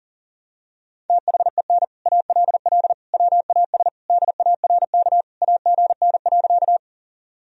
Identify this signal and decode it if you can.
Morse code, heard as a keyed tone, about 40 wpm, 700 Hz. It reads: THEN ALL WAS DARK AGN.
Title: Sentences from Top 500 Words and Abbreviations 40wpm